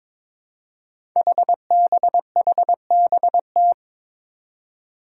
Morse code, heard as a keyed tone, about 22 wpm, 700 Hz. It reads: HBHBT
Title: Five Challenging Letter-Number Combinations 22wpm